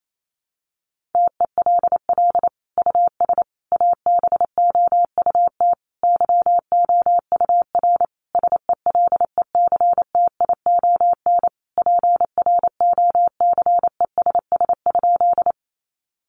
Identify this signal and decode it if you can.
Morse code, heard as a keyed tone, about 28 wpm, 700 Hz. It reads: TELL US ABOUT YOUR SELECTION PROCESS?
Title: Sentences from Top 1400 Words 28wpm